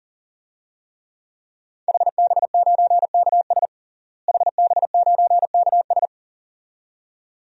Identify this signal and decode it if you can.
Morse code, heard as a keyed tone, about 40 wpm, 700 Hz. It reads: HB9KS HB9KS
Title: DXCC-100 Call Signs - 5 Characters - Repeated 2x - 40wpm